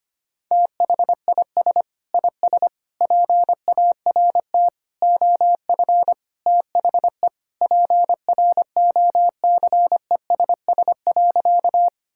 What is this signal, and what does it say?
Morse code, heard as a keyed tone, about 25 wpm, 700 Hz. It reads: THIS IS PART OF THE PROCESS.
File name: practice / 25wpm / Sentences with 2-Word Phrases - 25wpm